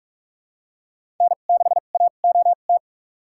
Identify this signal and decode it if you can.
Morse code, heard as a keyed tone, about 46 wpm, spaced as for 32 wpm, 700 Hz. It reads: N6AOT
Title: USA Call Signs - 5 Characters - 2x Character Spacing - 45wpm